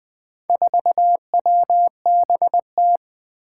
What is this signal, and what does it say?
Morse code, heard as a keyed tone, about 20 wpm, 700 Hz. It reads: 4WBT
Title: Four Challenging Letter-Number Combinations 20wpm